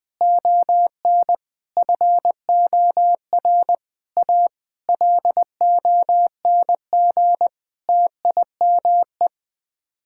Morse code, heard as a keyed tone, about 20 wpm, 700 Hz. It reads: ON FOR A LONG TIME
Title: Sets of 8 Words - 20wpm